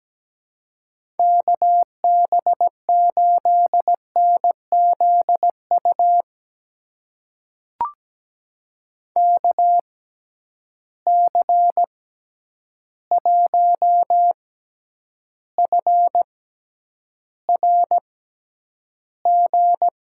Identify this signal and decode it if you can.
Morse code, heard as a keyed tone, about 17 wpm, spaced as for 9 wpm, 700 Hz. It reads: KB8NZU KC1FRG
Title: USA Call Signs - 6 Characters - 6x Character Spacing - 17wpm